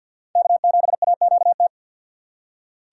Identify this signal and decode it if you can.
Morse code, heard as a keyed tone, about 52 wpm, 700 Hz. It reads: K7AQT
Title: USA Call Signs - 5 Characters - 4x Character Spacing - 50wpm